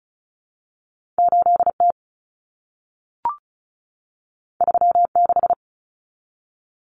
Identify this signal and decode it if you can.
Morse code, heard as a keyed tone, about 36 wpm, 700 Hz. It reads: 8T 36
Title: Common Cut Numbers - 2-Digit 35wpm